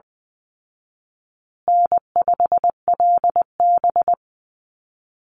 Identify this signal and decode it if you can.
Morse code, heard as a keyed tone, about 20 wpm, 700 Hz. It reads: N5LB
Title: Easy Call Signs Repeated 2x - 20wpm